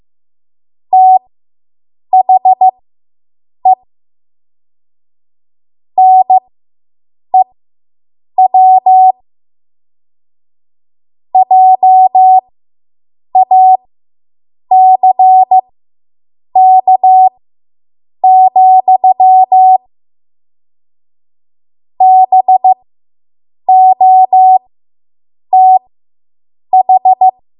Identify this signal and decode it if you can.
Morse code, heard as a keyed tone, about 15 wpm, spaced as for 7 wpm, 750 Hz. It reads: THE NEW JACK, BOTH